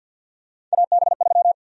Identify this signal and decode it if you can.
Morse code, heard as a keyed tone, about 53 wpm, 700 Hz. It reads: AB3
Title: USA Call Sign Prefixes - 50wpm